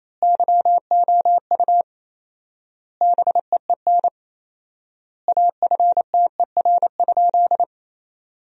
Morse code, heard as a keyed tone, about 28 wpm, 700 Hz. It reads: YOU BEEN AFTER?
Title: 4x Word-Spacing - Sentences from Top 200 Words - 28wpm